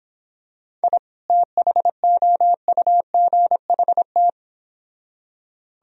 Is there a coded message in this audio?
Yes: Morse code from 0.8 to 4.3 s, heard as a keyed tone, about 25 wpm, 700 Hz. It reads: I THOUGHT